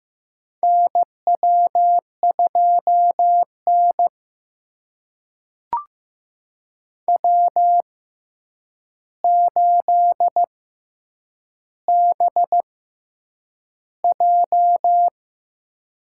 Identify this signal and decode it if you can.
Morse code, heard as a keyed tone, about 15 wpm, spaced as for 8 wpm, 700 Hz. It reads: NW2N W8BJ
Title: USA Call Signs - 4 Characters - 6x Character Spacing - 15wpm